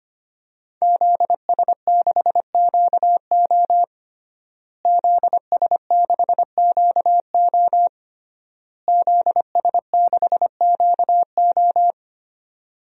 Morse code, heard as a keyed tone, about 25 wpm, 700 Hz. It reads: ZS6QO ZS6QO ZS6QO
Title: DXCC-100 Call Signs - 5 Characters - Repeated 3x - 25wpm